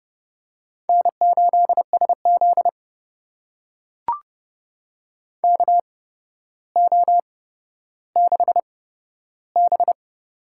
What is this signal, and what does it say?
Morse code, heard as a keyed tone, about 32 wpm, spaced as for 13 wpm, 700 Hz. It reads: N8SZ KO6B
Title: USA Call Signs - 4 Characters - 8x Character Spacing - 30wpm